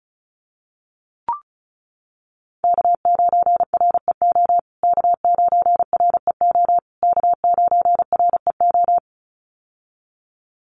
Morse code, heard as a keyed tone, about 34 wpm, 700 Hz. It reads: K9REO K9REO K9REO
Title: USA Call Signs Repeated 3x 35wpm